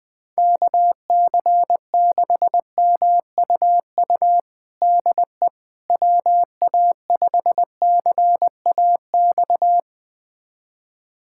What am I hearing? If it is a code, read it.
Morse code, heard as a keyed tone, about 20 wpm, 700 Hz. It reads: KC6MUU DE WA5CAX
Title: USA Callsign DE Callsign - 6 Characters - 20wpm